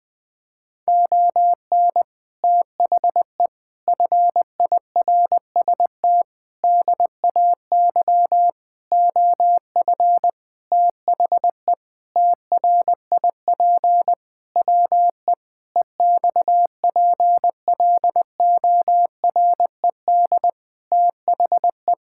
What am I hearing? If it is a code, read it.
Morse code, heard as a keyed tone, about 20 wpm, 700 Hz. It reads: ON THE FIRST DAY OF THE TRIP WE EXPLORED THE
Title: Sentences with 6-Word Phrases - Spoken Beginning - 20wpm